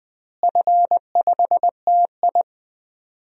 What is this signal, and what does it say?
Morse code, heard as a keyed tone, about 20 wpm, 700 Hz. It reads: F5TI
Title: DXCC-100 Call Signs - 4 Characters - Repeated 3x - 20wpm